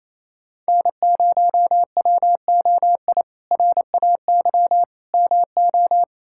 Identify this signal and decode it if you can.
Morse code, heard as a keyed tone, about 28 wpm, 700 Hz. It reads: N0WOI RAY MO
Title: K1USN SST QSO Practice - 28wpm